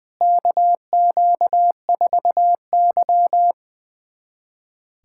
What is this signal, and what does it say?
Morse code, heard as a keyed tone, about 20 wpm, 700 Hz. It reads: KQ4Y
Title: USA Callsign DE Callsign - 4 Characters - 20wpm